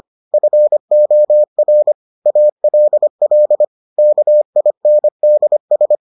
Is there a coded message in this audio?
Yes: Morse code from 0.3 to 6.0 s, heard as a keyed tone, about 25 wpm, 600 Hz. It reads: FOR ALL KINDS